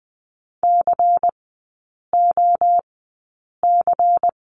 Morse code, heard as a keyed tone, about 20 wpm, spaced as for 8 wpm, 700 Hz. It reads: COC
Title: 172 Lesson 28e - Number 9 - Three Characters 20wpm